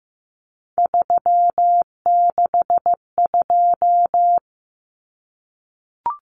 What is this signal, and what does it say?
Morse code, heard as a keyed tone, about 15 wpm, 700 Hz. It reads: 362
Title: Common Cut Numbers - 3-Digit 15wpm